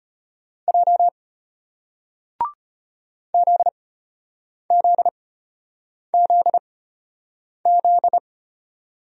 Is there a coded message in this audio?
Yes: Morse code from 0.7 to 8.2 s, heard as a keyed tone, about 34 wpm, spaced as for 8 wpm, 700 Hz. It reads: J ZZZZ